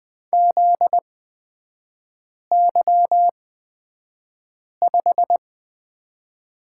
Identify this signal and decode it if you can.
Morse code, heard as a keyed tone, about 20 wpm, spaced as for 5 wpm, 700 Hz. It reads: ZY5
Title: 250 Lesson 40e - BK - Three Characters 20wpm